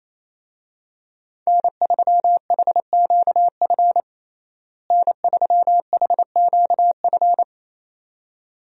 Morse code, heard as a keyed tone, about 27 wpm, 700 Hz. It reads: N3HQF N3HQF
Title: USA Call Signs - 5 Characters - Repeated 2x - 28wpm